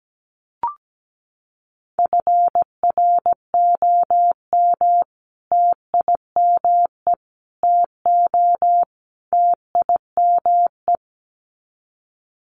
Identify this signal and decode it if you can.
Morse code, heard as a keyed tone, about 17 wpm, 700 Hz. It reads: FROM TIME TO TIME
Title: Sets of 4 Words 17wpm